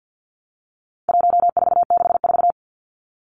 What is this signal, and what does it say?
Morse code, heard as a keyed tone, about 50 wpm, 700 Hz. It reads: 1464
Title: Numbers - 4-Digit 50wpm